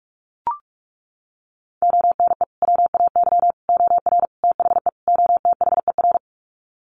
Morse code, heard as a keyed tone, about 44 wpm, 700 Hz. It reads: ONE WAY OR THE OTHER
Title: Sets of 5 Words 45wpm